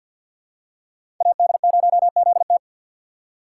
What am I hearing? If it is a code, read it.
Morse code, heard as a keyed tone, about 46 wpm, 700 Hz. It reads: AD0ZT
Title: USA Call Signs - 5 Characters - 6x Character Spacing - 50wpm